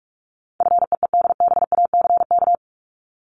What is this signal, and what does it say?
Morse code, heard as a keyed tone, about 44 wpm, 700 Hz. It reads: FEEDBACK
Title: Top 1501-1750 Words 45wpm